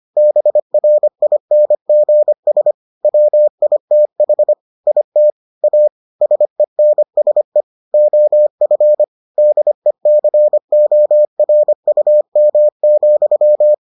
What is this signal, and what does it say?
Morse code, heard as a keyed tone, about 25 wpm, 600 Hz. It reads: BRINGS WITH IT A SENSE OF DECORUM,